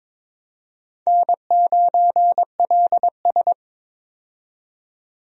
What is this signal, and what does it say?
Morse code, heard as a keyed tone, about 22 wpm, 700 Hz. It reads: N9LS
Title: USA Call Signs - 4 Characters - 22wpm